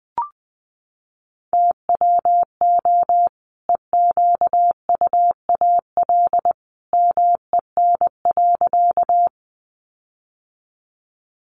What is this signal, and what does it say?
Morse code, heard as a keyed tone, about 20 wpm, 700 Hz. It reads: TWO EQUAL MEN.
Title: Sentences from Top 1400 Words 20wpm